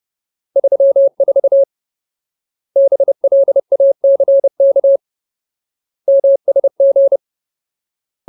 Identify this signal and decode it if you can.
Morse code, heard as a keyed tone, about 30 wpm, 550 Hz. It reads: 34 BLACK MSG
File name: Ultimate Word List / 30WPM Ultimate